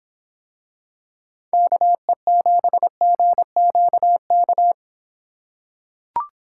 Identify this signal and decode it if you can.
Morse code, heard as a keyed tone, about 24 wpm, 700 Hz. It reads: KE7GQK